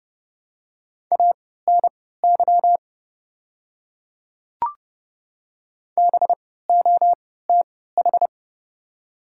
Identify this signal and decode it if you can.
Morse code, heard as a keyed tone, about 30 wpm, spaced as for 17 wpm, 700 Hz. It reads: ANY BOTH
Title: 3x Character Spacing - Top 300 Words - 30wpm